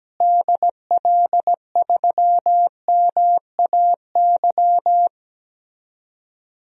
Morse code, heard as a keyed tone, about 17 wpm, 700 Hz. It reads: DL3MAY
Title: DXCC-100 Call Signs - 6 Characters - Repeated 3x - 17wpm